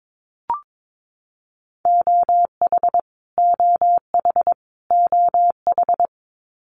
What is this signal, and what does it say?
Morse code, heard as a keyed tone, about 22 wpm, 700 Hz. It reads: OH OH OH
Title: Top 500 Words Repeated 22wpm